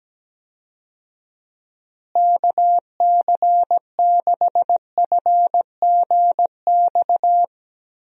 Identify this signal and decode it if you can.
Morse code, heard as a keyed tone, about 17 wpm, 700 Hz. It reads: KC6FGX
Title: USA Call Signs - 6 Characters - Repeated 3x - 17wpm